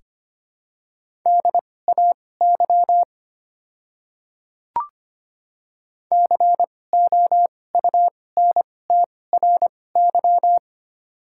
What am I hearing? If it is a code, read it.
Morse code, heard as a keyed tone, about 25 wpm, spaced as for 18 wpm, 690 Hz. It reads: DAY COUNTRY
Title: 2x Character Spacing - Top 300 Words - 25wpm